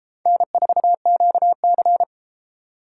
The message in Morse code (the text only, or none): N4QC